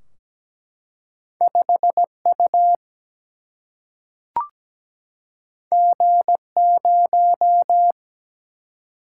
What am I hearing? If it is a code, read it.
Morse code, heard as a keyed tone, about 17 wpm, 700 Hz. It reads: 5 U G 0
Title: Single-Letter Single-Number 17wpm